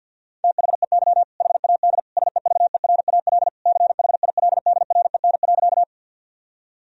Morse code, heard as a keyed tone, about 52 wpm, 700 Hz. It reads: THEY HAD SEVERAL CHILDREN.
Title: Sentences from Top 1000 Words - Review Part 2 - 50wpm